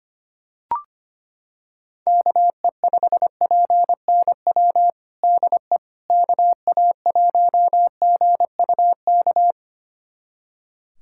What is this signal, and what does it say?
Morse code, heard as a keyed tone, about 25 wpm, 700 Hz. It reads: KE5PNW DE KA1GUK
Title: USA Callsign DE Callsign - 6 Characters - 25wpm